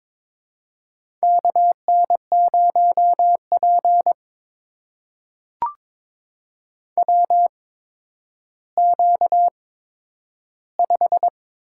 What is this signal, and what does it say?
Morse code, heard as a keyed tone, about 22 wpm, spaced as for 11 wpm, 700 Hz. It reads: KN0P WQ5
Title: USA Call Signs - 4 Characters - 8x Character Spacing - 22wpm